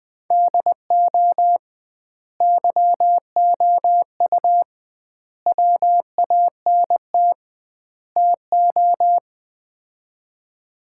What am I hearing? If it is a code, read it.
Morse code, heard as a keyed tone, about 20 wpm, 700 Hz. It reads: DO YOU WANT TO
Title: 2x Word-Spacing - Sets of 4 Words - 20wpm